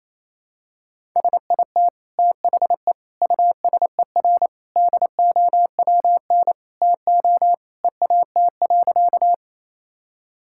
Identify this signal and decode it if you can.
Morse code, heard as a keyed tone, about 28 wpm, 700 Hz. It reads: SIT THE USER DOWN TO EAT.
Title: Sentences from Top 1000 Words - Review Part 5 - 28wpm